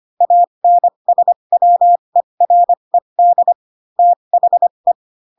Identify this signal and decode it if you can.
Morse code, heard as a keyed tone, about 25 wpm, spaced as for 22 wpm, 700 Hz. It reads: ANSWERED THE
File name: Pinocchio / Pinocchio ch6 18 25 ik1hge